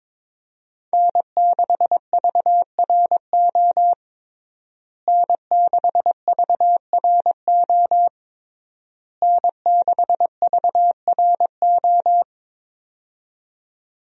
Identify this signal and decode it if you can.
Morse code, heard as a keyed tone, about 22 wpm, 700 Hz. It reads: N6VRO N6VRO N6VRO
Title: USA Call Signs - 5 Characters - Repeated 3x - 22wpm